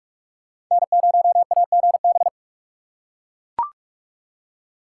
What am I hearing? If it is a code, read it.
Morse code, heard as a keyed tone, about 43 wpm, 700 Hz. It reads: N0AGB